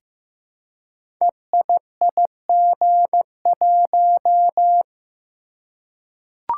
Morse code, heard as a keyed tone, about 15 wpm, 700 Hz. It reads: EIIG1
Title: Code Groups - 5 Letter:Numbers - 15wpm